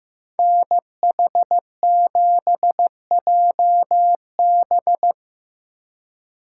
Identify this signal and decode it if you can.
Morse code, heard as a keyed tone, about 17 wpm, 700 Hz. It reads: NH7JB
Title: Speed Racing - USA 2x2 Call Signs 15wpm